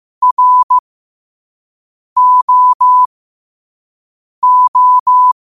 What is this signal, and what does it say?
Morse code, heard as a keyed tone, about 15 wpm, spaced as for 5 wpm, 1000 Hz. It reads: ROO
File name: qotdfiles / 5wpm